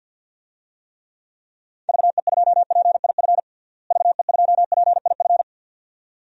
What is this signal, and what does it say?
Morse code, heard as a keyed tone, about 52 wpm, 700 Hz. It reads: VE2PIF VE2PIF
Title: DXCC-100 Call Signs - 6 Characters - Repeated 2x - 50wpm